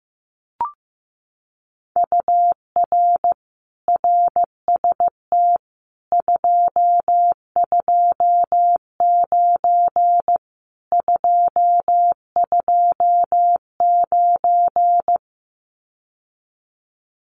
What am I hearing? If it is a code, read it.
Morse code, heard as a keyed tone, about 15 wpm, 700 Hz. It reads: UR RST 229 229
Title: Common QSO Phrases Encore 15wpm